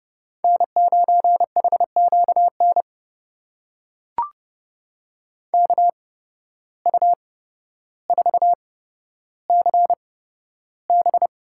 Morse code, heard as a keyed tone, about 30 wpm, spaced as for 13 wpm, 700 Hz. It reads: N9HQN KU4CB
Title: USA Call Signs - 5 Characters - 8x Character Spacing - 30wpm